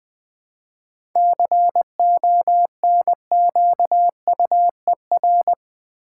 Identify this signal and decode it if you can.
Morse code, heard as a keyed tone, about 20 wpm, 700 Hz. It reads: CONQUER
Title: 7 Letter Words - First 3 Letters Given - 20wpm